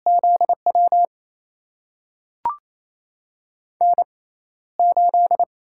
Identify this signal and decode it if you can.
Morse code, heard as a keyed tone, about 28 wpm, spaced as for 14 wpm, 700 Hz. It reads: ZW N8
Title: USA Call Signs - 6x Character Spacing - 28wpm